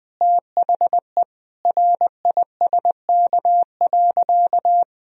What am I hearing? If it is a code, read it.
Morse code, heard as a keyed tone, about 20 wpm, 700 Hz. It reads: THE RISK.
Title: Sentences from Top 1000 Words - Review Part 4 - 20wpm